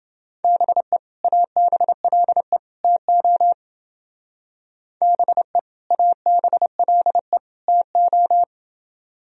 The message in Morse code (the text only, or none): BE ABLE TO BE ABLE TO